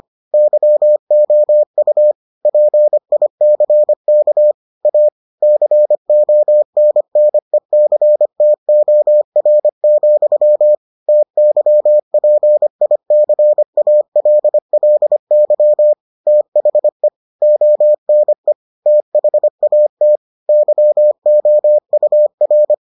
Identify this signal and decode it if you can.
Morse code, heard as a keyed tone, about 25 wpm, 600 Hz. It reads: YOU PICK A CONNECTOR, TYPICALLY THE ONE THAT YOUR